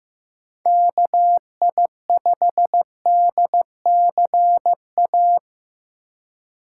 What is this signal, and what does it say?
Morse code, heard as a keyed tone, about 15 wpm, 700 Hz. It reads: KI5DCA
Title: USA Call Signs - 6 Characters - 2x Character Spacing - 15wpm